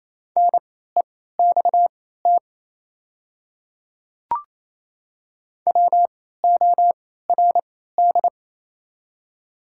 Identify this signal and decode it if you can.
Morse code, heard as a keyed tone, about 28 wpm, spaced as for 16 wpm, 700 Hz. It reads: NEXT WORD
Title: 3x Character Spacing - Top 300 Words - 28wpm